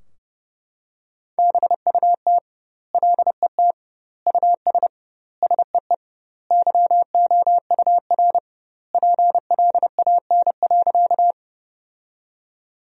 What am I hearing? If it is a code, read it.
Morse code, heard as a keyed tone, about 30 wpm, 700 Hz. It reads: BUT LET US SEE YOUR PLAN.
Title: 2x Word-Spacing - Sentences from Top 500 Words - 30wpm